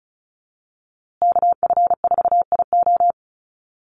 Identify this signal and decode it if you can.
Morse code, heard as a keyed tone, about 34 wpm, 700 Hz. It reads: KF4IO
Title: USA Call Signs Repeated 2x 35wpm